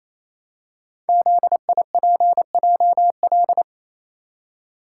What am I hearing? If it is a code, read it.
Morse code, heard as a keyed tone, about 27 wpm, 700 Hz. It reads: ZIPJL
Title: Code Groups - 5 Letter:Numbers - 28wpm